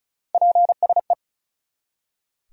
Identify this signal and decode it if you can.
Morse code, heard as a keyed tone, about 47 wpm, spaced as for 42 wpm, 700 Hz. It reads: PSE
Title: Speed-Racing Abbreviations - 35wpm